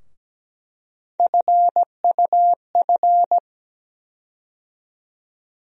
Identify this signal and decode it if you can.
Morse code, heard as a keyed tone, about 17 wpm, 700 Hz. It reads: FUF
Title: USA Call Sign Postfixes - 3 Letters 17wpm